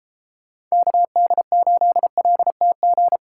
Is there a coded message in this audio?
Yes: Morse code from 0.7 to 3.2 s, heard as a keyed tone, about 31 wpm, 700 Hz. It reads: KD8LTG